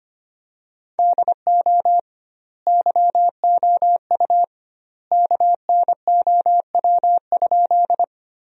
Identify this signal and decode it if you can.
Morse code, heard as a keyed tone, about 25 wpm, 700 Hz. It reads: DO YOU KNOW?
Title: 2x Word-Spacing - Sentences from Top 100 Words - 25wpm